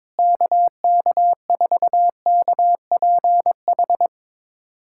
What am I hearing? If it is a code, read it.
Morse code, heard as a keyed tone, about 22 wpm, 700 Hz. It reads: KK4KPH
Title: USA Callsign DE Callsign 22wpm